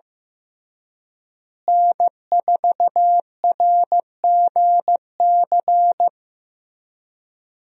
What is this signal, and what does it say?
Morse code, heard as a keyed tone, about 15 wpm, 700 Hz. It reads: N4RGC
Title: USA Call Signs - 5 Characters - Repeated 3x - 15wpm